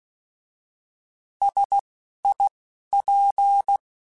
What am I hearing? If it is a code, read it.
Morse code, heard as a keyed tone, about 16 wpm, spaced as for 11 wpm, 800 Hz. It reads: SIP
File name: Alphabetical / Alphabetical mp3s / Ss @10